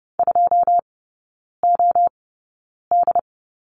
Morse code, heard as a keyed tone, about 30 wpm, spaced as for 9 wpm, 700 Hz. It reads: 2OD
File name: Lesson 16e - Letter c - Three Characters 30wpm